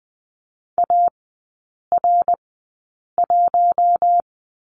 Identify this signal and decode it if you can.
Morse code, heard as a keyed tone, about 20 wpm, spaced as for 8 wpm, 700 Hz. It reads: AR1
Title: 053 Lesson 10e - Letter r - Three Characters 20wpm